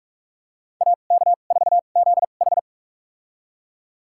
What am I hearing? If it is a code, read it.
Morse code, heard as a keyed tone, about 44 wpm, spaced as for 32 wpm, 700 Hz. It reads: AK4ZH
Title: USA Call Signs - 5 Characters - 2x Character Spacing - 45wpm